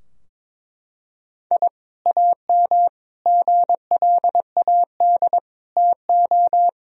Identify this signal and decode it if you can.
Morse code, heard as a keyed tone, about 22 wpm, 700 Hz. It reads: I AM GLAD TO
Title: Sentences from Top 1000 Words - Review Part 4 - 22wpm